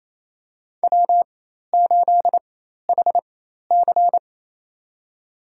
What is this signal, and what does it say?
Morse code, heard as a keyed tone, about 28 wpm, spaced as for 13 wpm, 700 Hz. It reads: W8HC
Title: USA Call Signs - 4 Characters - 4x Character Spacing - 28wpm